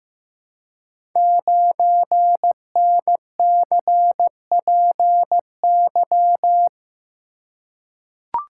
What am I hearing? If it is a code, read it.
Morse code, heard as a keyed tone, about 15 wpm, 700 Hz. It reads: 9NCPY
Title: Code Groups - 5 Letter:Numbers - 15wpm